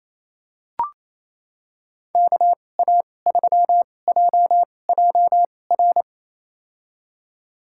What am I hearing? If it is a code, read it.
Morse code, heard as a keyed tone, about 27 wpm, spaced as for 20 wpm, 700 Hz. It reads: KA3JJR
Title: USA Call Signs - 6 Characters - 2x Character Spacing - 28wpm